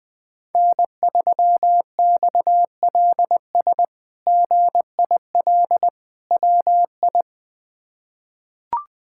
Morse code, heard as a keyed tone, about 20 wpm, 700 Hz. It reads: N3XLS GIL WI
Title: K1USN SST QSO Practice - 20wpm